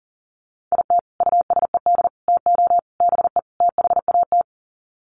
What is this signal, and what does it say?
Morse code, heard as a keyed tone, about 40 wpm, 700 Hz. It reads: IT USED TO BE THAT
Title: Sets of 5 Words Encore2 40wpm